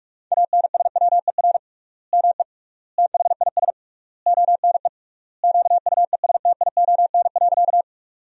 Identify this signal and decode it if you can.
Morse code, heard as a keyed tone, about 46 wpm, 700 Hz. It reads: ANSWER ME THIS ONE QUESTION.
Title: 3x Word-Spacing - Sentences from Top 400 Words - 45wpm